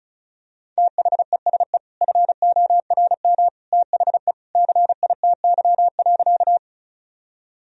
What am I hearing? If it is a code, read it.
Morse code, heard as a keyed tone, about 34 wpm, 700 Hz. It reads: THESE FORM THE CITY.
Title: Sentences from Top 1000 Words - Review Part 1 - 35wpm